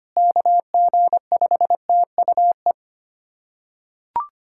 Morse code, heard as a keyed tone, about 25 wpm, 700 Hz. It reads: KG5TUE
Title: USA Callsign DE Callsign 25wpm